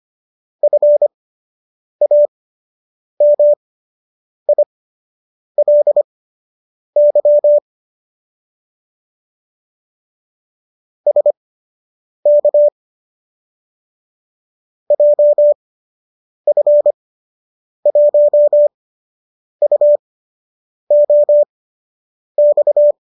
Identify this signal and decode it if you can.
Morse code, heard as a keyed tone, about 25 wpm, spaced as for 8 wpm, 590 Hz. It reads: FAMILY SK JF1UOX